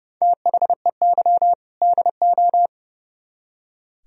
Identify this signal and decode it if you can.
Morse code, heard as a keyed tone, about 34 wpm, 700 Hz. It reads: THEY DO
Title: Speed-Racing - Sets of 2 Words 30wpm